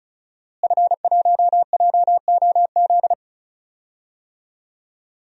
Morse code, heard as a keyed tone, about 36 wpm, 700 Hz. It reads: F1JOZ
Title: DXCC-100 Call Signs - 5 Characters - Repeated 3x - 35wpm